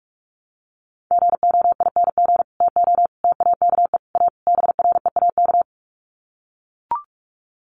Morse code, heard as a keyed tone, about 44 wpm, 700 Hz. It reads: GOING TO TAKE A BREAK
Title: Sets of 5 Words 45wpm